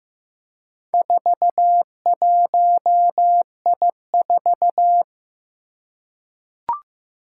Morse code, heard as a keyed tone, about 15 wpm, 700 Hz. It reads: 41I4
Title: Four Challenging Letter-Number Combinations 15wpm